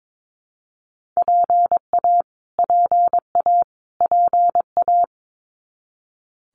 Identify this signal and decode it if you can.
Morse code, heard as a keyed tone, about 22 wpm, 700 Hz. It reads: PA PA PA
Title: US States Repeated 3x 22wpm